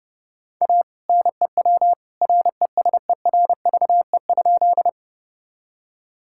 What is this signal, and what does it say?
Morse code, heard as a keyed tone, about 30 wpm, 700 Hz. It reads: A NEW RESERVE?
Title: Sentences from Top 1000 Words - Review Part 6 - 30wpm